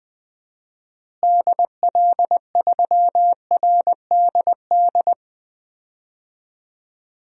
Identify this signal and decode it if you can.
Morse code, heard as a keyed tone, about 20 wpm, 700 Hz. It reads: DL3RDD
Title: DXCC-100 Call Signs - 6 Characters - Repeated 3x - 20wpm